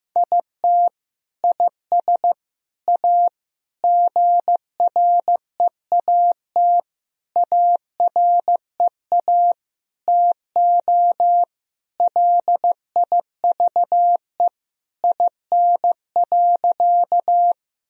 Morse code, heard as a keyed tone, about 16 wpm, 700 Hz. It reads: IT IS A GREAT AREA TO LIVE IN.
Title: Sentences from Top 1000 Words - Review Part 2 - 15wpm